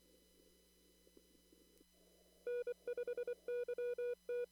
Morse code, heard as a keyed tone, about 24 wpm, 490 Hz. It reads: N5YT
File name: CW / kt5j wpx cw-op-k5pi